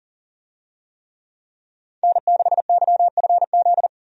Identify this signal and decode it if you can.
Morse code, heard as a keyed tone, about 39 wpm, 700 Hz. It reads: N6YFZ
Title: USA Call Signs - 5 Characters - 40wpm